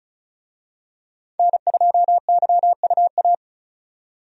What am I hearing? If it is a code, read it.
Morse code, heard as a keyed tone, about 34 wpm, 700 Hz. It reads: N2YUA
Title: USA Call Signs - 5 Characters - 35wpm